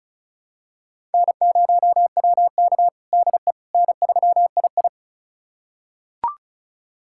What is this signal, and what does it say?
Morse code, heard as a keyed tone, about 34 wpm, 700 Hz. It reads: N0WK DE N3II
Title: USA Callsign DE Callsign - 4 Characters - 35wpm